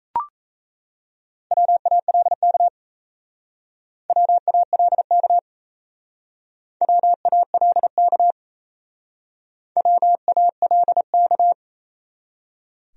Morse code, heard as a keyed tone, about 35 wpm, 700 Hz. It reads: WALK WALK WALK WALK